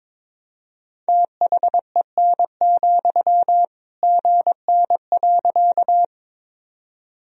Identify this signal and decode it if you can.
Morse code, heard as a keyed tone, about 22 wpm, 700 Hz. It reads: THEN, GN.
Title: Sentences from Top 300 Words and Abbreviations 22wpm